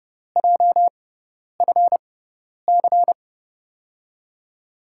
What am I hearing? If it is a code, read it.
Morse code, heard as a keyed tone, about 30 wpm, spaced as for 10 wpm, 700 Hz. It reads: JFC
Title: USA Call Signs - 5 Characters - 6x Character Spacing - 30wpm